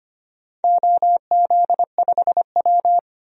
Morse code, heard as a keyed tone, about 25 wpm, 700 Hz. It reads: OZ5W
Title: DXCC-100 Call Signs - 4 Characters - Repeated 2x - 25wpm